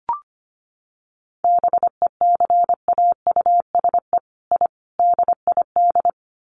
Morse code, heard as a keyed tone, about 25 wpm, 700 Hz. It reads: BECAUSE I DID
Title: Sets of 5 Words Encore2 25wpm